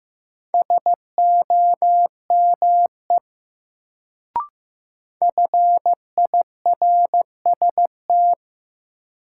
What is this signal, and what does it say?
Morse code, heard as a keyed tone, about 15 wpm, 700 Hz. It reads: SOME FIRST
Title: Sentences from Top 100 Words - New Word Drill - 15wpm